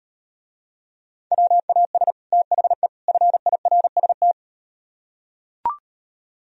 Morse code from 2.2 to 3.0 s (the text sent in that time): THE